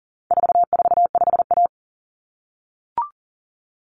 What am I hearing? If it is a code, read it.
Morse code, heard as a keyed tone, about 40 wpm, 700 Hz. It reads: 445A
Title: Common Cut Numbers - 4-Digit 40wpm